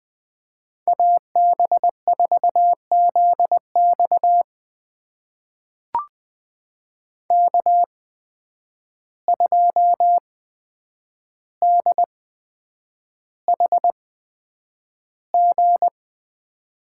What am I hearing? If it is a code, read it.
Morse code, heard as a keyed tone, about 20 wpm, spaced as for 9 wpm, 700 Hz. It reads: AB4ZX K2DHG
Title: USA Call Signs - 5 Characters - 8x Character Spacing - 20wpm